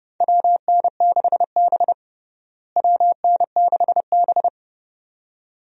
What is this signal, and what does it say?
Morse code, heard as a keyed tone, about 32 wpm, 700 Hz. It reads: WN6B WN6B
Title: USA Call Signs - 4 Characters - Repeated 2x - 30wpm